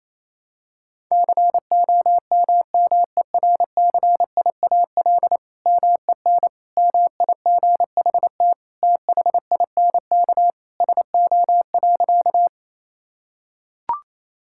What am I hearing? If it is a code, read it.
Morse code, heard as a keyed tone, about 28 wpm, 700 Hz. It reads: COMMERCIAL MEN MIGHT THINK SO.
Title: Sentences from Top 1000 Words - Review Part 5 - 28wpm